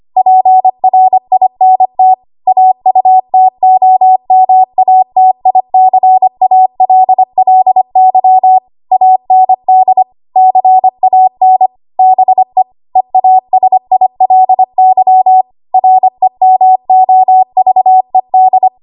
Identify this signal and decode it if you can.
Morse code, heard as a keyed tone, about 25 wpm, 750 Hz. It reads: PRINT AUTOMATICALLY AND CAN BE EASILY REMOVED
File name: Archive / 25 WPM / 25WPM